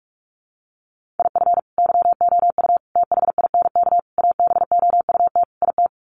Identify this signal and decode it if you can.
Morse code, heard as a keyed tone, about 44 wpm, 690 Hz. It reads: IF YOU THINK ABOUT IT
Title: Sets of 5 Words Encore2 45wpm